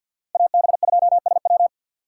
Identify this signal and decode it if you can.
Morse code, heard as a keyed tone, about 52 wpm, 690 Hz. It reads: AB2SW
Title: USA Call Signs - 5 Characters - 3x Character Spacing - 50wpm